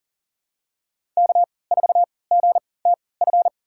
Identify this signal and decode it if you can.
Morse code, heard as a keyed tone, about 40 wpm, spaced as for 23 wpm, 700 Hz. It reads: K4GTF
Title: USA Call Signs - 5 Characters - 3x Character Spacing - 40wpm